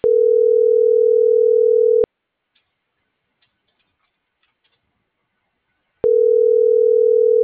Ringback tone (North America), 440 + 480 Hz.